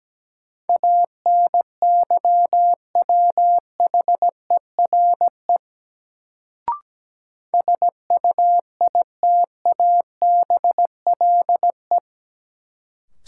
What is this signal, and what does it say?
Morse code, heard as a keyed tone, about 17 wpm, 700 Hz. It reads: ANYWHERE SUITABLE
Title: Sentences from Top 1400 Words - New Word Drill - 17wpm